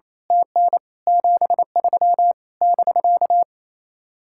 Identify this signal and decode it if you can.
Morse code, heard as a keyed tone, about 27 wpm, 700 Hz. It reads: TN 73 <BK>
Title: Speed-Racing Common QSO Phrases - 28wpm